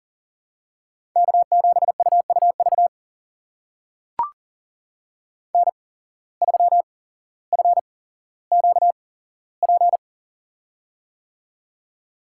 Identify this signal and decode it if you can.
Morse code, heard as a keyed tone, about 40 wpm, spaced as for 17 wpm, 700 Hz. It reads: K7UUV N3FQP
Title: USA Call Signs - 5 Characters - 8x Character Spacing - 40wpm